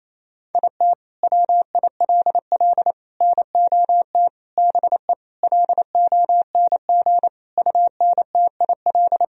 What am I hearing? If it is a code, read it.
Morse code, heard as a keyed tone, about 28 wpm, 700 Hz. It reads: IT WILL NOT BE LONG UNTIL
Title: Sentences from Top 1000 Words - Review Part 1 - 28wpm